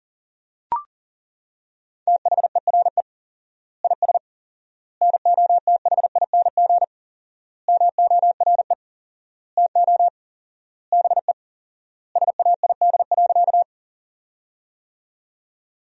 Morse code, heard as a keyed tone, about 40 wpm, 700 Hz. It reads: THERE IS NOTHING MORE TO BE SAID.
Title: 4x Word-Spacing - Sentences from Top 500 Words - 40wpm